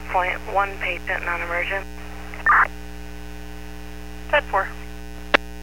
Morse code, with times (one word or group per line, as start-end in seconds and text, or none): none